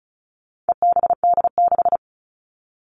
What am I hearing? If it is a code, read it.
Morse code, heard as a keyed tone, about 34 wpm, 700 Hz. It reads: EBD6
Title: All Cut Numbers - 4-Digit 35wpm